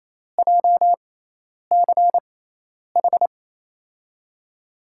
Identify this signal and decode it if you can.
Morse code, heard as a keyed tone, about 28 wpm, spaced as for 10 wpm, 700 Hz. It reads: JCH
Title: USA Call Signs - 6 Characters - 6x Character Spacing - 28wpm